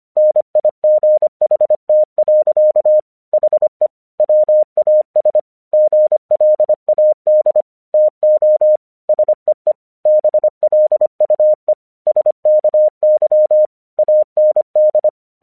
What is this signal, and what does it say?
Morse code, heard as a keyed tone, about 25 wpm, 610 Hz. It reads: NIGHT. HE WAS GLAD TO SEE BLUE SKY AND
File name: English 25wpm